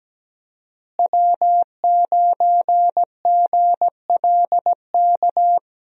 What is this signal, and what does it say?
Morse code, heard as a keyed tone, about 17 wpm, 700 Hz. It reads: W9GLK
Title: USA Call Signs - 5 Characters - Repeated 2x - 17wpm